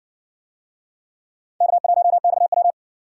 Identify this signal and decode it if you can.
Morse code, heard as a keyed tone, about 60 wpm, 700 Hz. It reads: K1XW